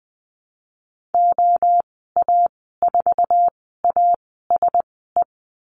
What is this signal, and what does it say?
Morse code, heard as a keyed tone, about 20 wpm, spaced as for 15 wpm, 700 Hz. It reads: OA4ASE
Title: 048 Lesson 9g - Number 4 - Call Signs 20wpm